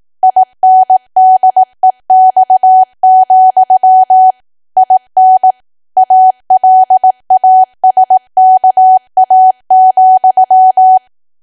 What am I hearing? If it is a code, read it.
Morse code, heard as a keyed tone, about 18 wpm, 750 Hz. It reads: INDEX, IN ALASKA,
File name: cwa / W1AW-Level2 / 18WPM